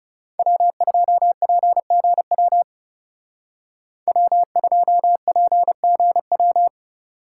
Morse code, heard as a keyed tone, about 39 wpm, 700 Hz. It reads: W2PGW W2PGW